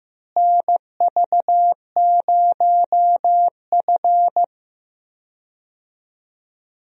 Morse code, heard as a keyed tone, about 15 wpm, 700 Hz. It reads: NV0F
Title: USA Call Signs - 4 Characters - Repeated 3x - 15wpm